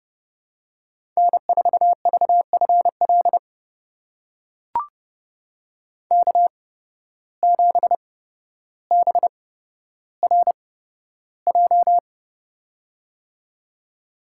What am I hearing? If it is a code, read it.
Morse code, heard as a keyed tone, about 30 wpm, spaced as for 13 wpm, 700 Hz. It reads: N4VFL K7BRJ
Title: USA Call Signs - 5 Characters - 8x Character Spacing - 30wpm